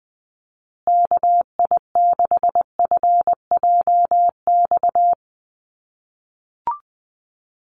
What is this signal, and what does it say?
Morse code, heard as a keyed tone, about 20 wpm, 700 Hz. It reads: KI6FJX
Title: USA Call Signs Repeated 3x 20wpm